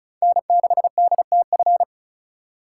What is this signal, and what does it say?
Morse code, heard as a keyed tone, about 36 wpm, 700 Hz. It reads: N6DTF
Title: USA Call Signs - 4x Character Spacing - 35wpm